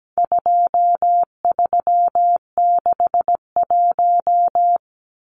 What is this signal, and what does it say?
Morse code, heard as a keyed tone, about 17 wpm, 700 Hz. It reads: 2361
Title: Numbers - 4-Digit 17wpm